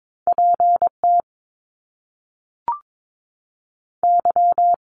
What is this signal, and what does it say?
Morse code, heard as a keyed tone, about 22 wpm, 700 Hz. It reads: P T Y
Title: Top Two Letter Combinations in Words 22wpm